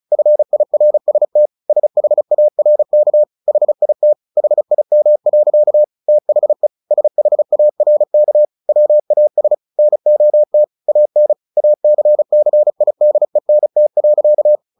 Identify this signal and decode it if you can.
Morse code, heard as a keyed tone, about 34 wpm, 600 Hz. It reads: FIRST SHARK HIT HIM. THE SHARK WAS NOT AN ACCIDENT.
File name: the old man and the sea 2 35wpm